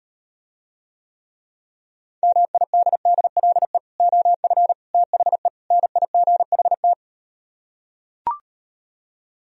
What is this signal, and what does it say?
Morse code, heard as a keyed tone, about 39 wpm, 700 Hz. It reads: MIDDLE OF THE NIGHT